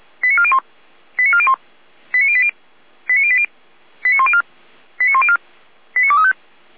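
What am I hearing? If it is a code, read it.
Five-tone selective call (ZVEI), heard as quick runs of five tones, tones of about 70 ms. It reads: 89401 89401 89080 89080 89104 89104 89235
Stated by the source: ZVEI five-tone selective call received with an SDR.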